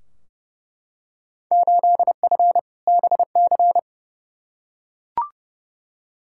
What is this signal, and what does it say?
Morse code, heard as a keyed tone, about 30 wpm, 700 Hz. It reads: 8F BC
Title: Field Day Exchanges Encore - 30wpm